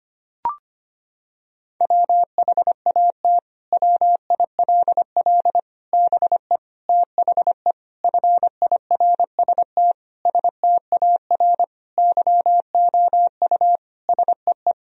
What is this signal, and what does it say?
Morse code, heard as a keyed tone, about 25 wpm, 700 Hz. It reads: WHAT WILL BE THE FIRST STAR YOU SEE
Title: Sentences from Top 1000 Words - Review Part 2 - 25wpm